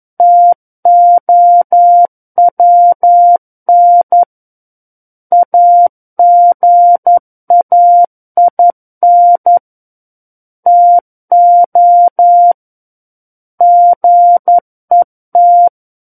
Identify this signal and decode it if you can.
Morse code, heard as a keyed tone, about 11 wpm, 700 Hz. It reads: TOWN AGAIN TO GET
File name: 11 WPM